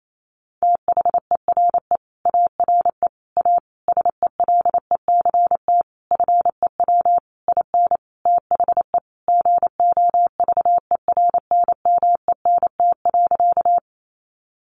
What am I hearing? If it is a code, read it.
Morse code, heard as a keyed tone, about 28 wpm, 700 Hz. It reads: THERE ARE A SELECT FEW IN THE GOVERNMENT.
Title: Sentences from Top 1300 Words 28wpm